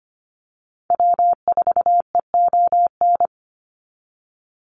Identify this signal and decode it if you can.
Morse code, heard as a keyed tone, about 25 wpm, 700 Hz. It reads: W4EON
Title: NA SOTA Activator Callsigns 25wpm